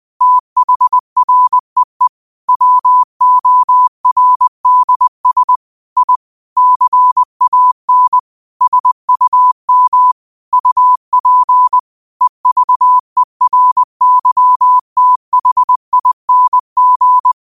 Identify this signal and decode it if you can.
Morse code, heard as a keyed tone, about 20 wpm, 1000 Hz. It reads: THREE WORDS I CAN SUM UP EVERYTHING
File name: qotdfiles / 20wpm